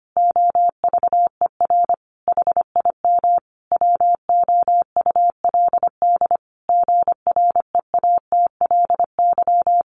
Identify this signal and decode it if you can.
Morse code, heard as a keyed tone, about 25 wpm, 700 Hz. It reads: OVER HIM WOULD GREATLY
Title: Sentences from Top 1400 Words 25wpm